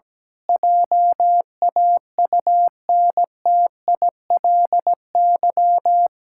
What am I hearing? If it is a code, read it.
Morse code, heard as a keyed tone, about 17 wpm, 700 Hz. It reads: JAUNTILY
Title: Pangram Challenge v2 - 17wpm